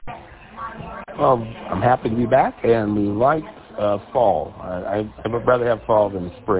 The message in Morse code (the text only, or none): none